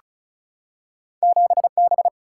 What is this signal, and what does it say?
Morse code, heard as a keyed tone, about 36 wpm, 700 Hz. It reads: 7B
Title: Single-Letter Single-Number 35wpm